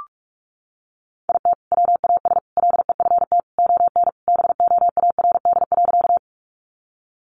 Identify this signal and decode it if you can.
Morse code, heard as a keyed tone, about 44 wpm, 700 Hz. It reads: IT WAS LEFT ON BOARD.
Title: Sentences from Top 700 Words 45wpm